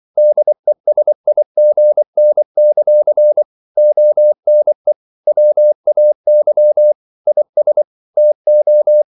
Morse code, heard as a keyed tone, about 24 wpm, 600 Hz. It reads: DESIGN; ONE WAY IS TO
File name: CW / Practice / EnglishQuotations0224